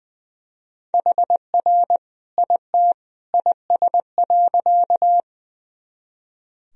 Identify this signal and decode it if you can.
Morse code, heard as a keyed tone, about 20 wpm, 700 Hz. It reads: HR IT IS.
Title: Sentences from Top 200 Words and Abbreviations 20wpm